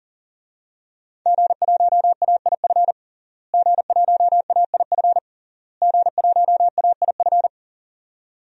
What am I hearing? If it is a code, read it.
Morse code, heard as a keyed tone, about 40 wpm, 700 Hz. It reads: G1AIF G1AIF G1AIF
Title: DXCC-100 Call Signs - 5 Characters - Repeated 3x - 40wpm